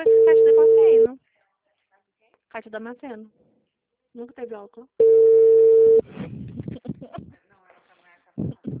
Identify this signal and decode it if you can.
Ringback tone (China, 450 Hz), with speech.